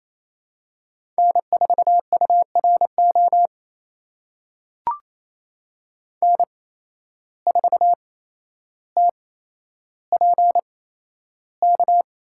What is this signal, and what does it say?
Morse code, heard as a keyed tone, about 28 wpm, spaced as for 12 wpm, 700 Hz. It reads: N4URO N4TPK
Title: USA Call Signs - 5 Characters - 8x Character Spacing - 28wpm